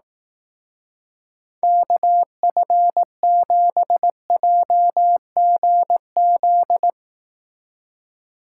Morse code, heard as a keyed tone, about 18 wpm, 700 Hz. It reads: KF7JGZ
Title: Speed Racing - USA 2x3 Call Signs 15wpm